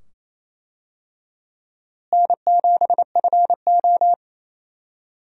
Morse code, heard as a keyed tone, about 28 wpm, 700 Hz. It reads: N7FO
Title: USA Call Signs - 4 Characters - Repeated 2x - 28wpm